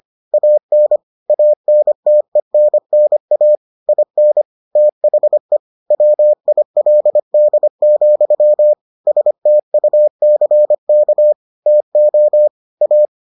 Morse code, heard as a keyed tone, about 25 wpm, 600 Hz. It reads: AN ANTENNA IN THE WILD, STUCK TO A